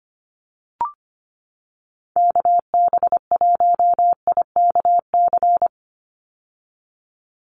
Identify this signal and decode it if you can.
Morse code, heard as a keyed tone, about 25 wpm, 700 Hz. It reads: KB1IKC
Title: USA Call Signs 25wpm